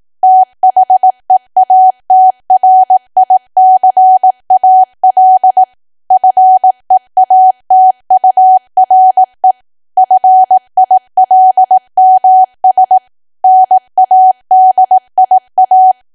Morse code, heard as a keyed tone, about 18 wpm, 750 Hz. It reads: THEATRICAL FEATURE FILMS NADIA
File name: Archive / 18 WPM / 18WPM